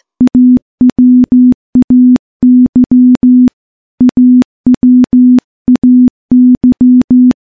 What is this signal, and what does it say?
Morse code, heard as a keyed tone, about 15 wpm, 250 Hz. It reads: AWAY AWAY
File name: morse / morse-training-15WPM